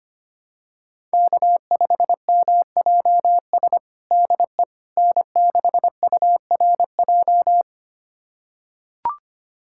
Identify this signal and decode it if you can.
Morse code, heard as a keyed tone, about 25 wpm, 700 Hz. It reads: K5MJS DE N6URJ
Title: USA Callsign DE Callsign - 5 Characters - 25wpm